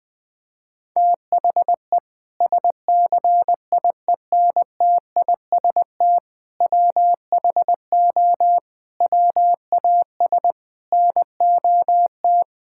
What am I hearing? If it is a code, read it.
Morse code, heard as a keyed tone, about 21 wpm, 700 Hz. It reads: THE SCIENTIST WHO WAS NOT
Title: Sentences with 8-Word Phrases - 20wpm